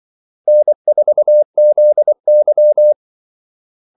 Morse code, heard as a keyed tone, about 24 wpm, 600 Hz. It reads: N4ZY
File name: Callsigns-24wpm